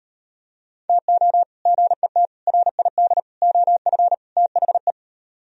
Morse code, heard as a keyed tone, about 38 wpm, 700 Hz. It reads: TO GET RID OF THE